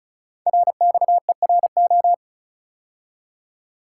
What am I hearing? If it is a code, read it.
Morse code, heard as a keyed tone, about 36 wpm, 700 Hz. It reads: RXERO
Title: Code Groups - 5 Letters - 35wpm